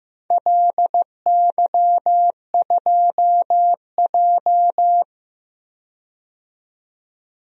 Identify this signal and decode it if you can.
Morse code, heard as a keyed tone, about 16 wpm, 700 Hz. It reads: LY2J
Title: DXCC-100 Call Signs - 4 Characters - 15wpm